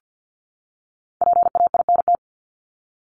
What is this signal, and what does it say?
Morse code, heard as a keyed tone, about 50 wpm, 700 Hz. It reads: PAINT
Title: Top 751-1000 Words 50wpm